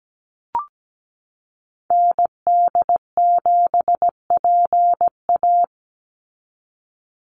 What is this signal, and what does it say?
Morse code, heard as a keyed tone, about 17 wpm, 700 Hz. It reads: ND7PA
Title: NA SOTA Activator Callsigns 17wpm